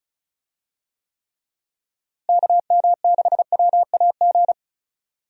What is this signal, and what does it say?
Morse code, heard as a keyed tone, about 34 wpm, 700 Hz. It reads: KM6WAG
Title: USA Call Signs - 6 Characters - Repeated 3x - 35wpm